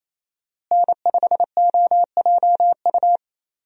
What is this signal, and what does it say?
Morse code, heard as a keyed tone, about 28 wpm, 700 Hz. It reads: N5OJU
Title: USA Call Signs - 5 Characters - Repeated 2x - 28wpm